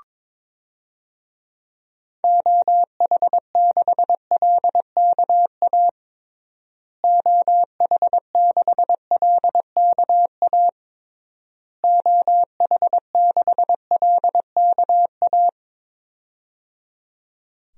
Morse code, heard as a keyed tone, about 22 wpm, 700 Hz. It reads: OH6LKA OH6LKA OH6LKA
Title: DXCC-100 Call Signs - 6 Characters - Repeated 3x - 22wpm